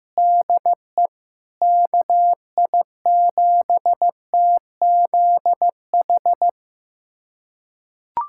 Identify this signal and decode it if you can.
Morse code, heard as a keyed tone, about 15 wpm, 700 Hz. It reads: DE KI7TZH
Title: USA Callsign DE Callsign - 6 Characters - 15wpm